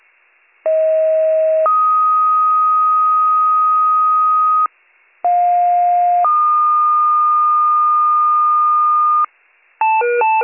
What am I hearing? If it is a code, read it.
Two-tone page, heard as a one-second tone, then a three-second tone, about 640 then 1250 Hz, about 710 then 1220 Hz. It is heard from 0.6 to 9.3 s.